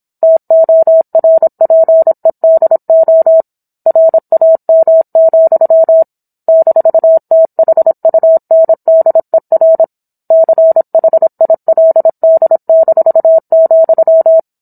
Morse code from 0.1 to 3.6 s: TORPEDO

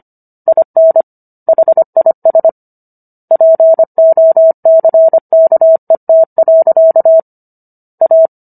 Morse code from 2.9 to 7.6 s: POCKET.

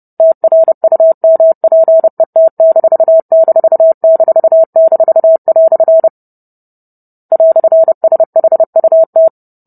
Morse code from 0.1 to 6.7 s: TRUMPET----"

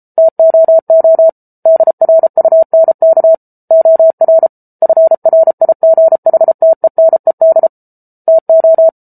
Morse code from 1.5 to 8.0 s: DRUNK OR FRIGHTENED

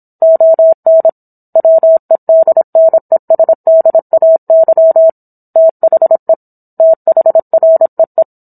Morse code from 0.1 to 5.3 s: ON WEDNESDAY

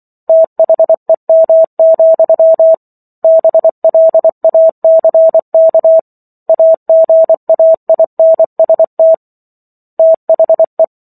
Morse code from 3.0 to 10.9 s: BLACK AGAINST THE

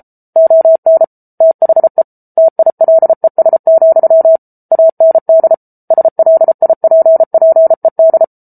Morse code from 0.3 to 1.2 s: ON

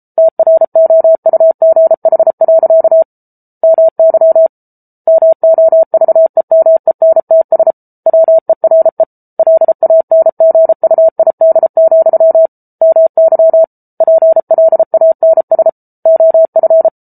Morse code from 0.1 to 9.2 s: TROUGH. MY MOVEMENTS WERE

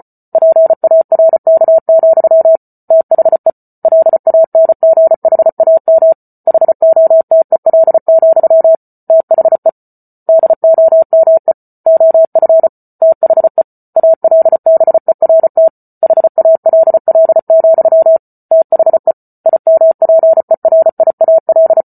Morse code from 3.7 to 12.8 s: LANGHAM HOTEL, THE DOME OF

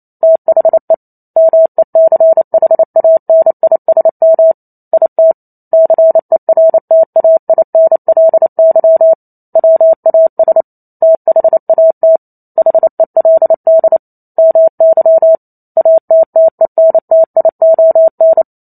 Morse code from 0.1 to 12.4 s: THE MECHANISM IT CERTAINLY WAS THAT

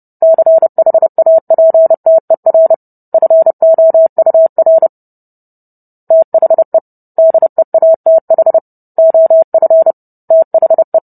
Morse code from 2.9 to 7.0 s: FOUR THE

